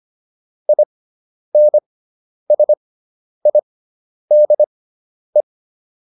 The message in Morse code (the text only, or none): INSIDE